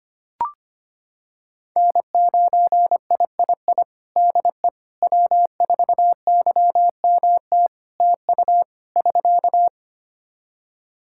Morse code, heard as a keyed tone, about 25 wpm, 700 Hz. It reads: N9III DE W4YMT TU <SK>
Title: Practice QSO - Exchange 4 - Ending - 25wpm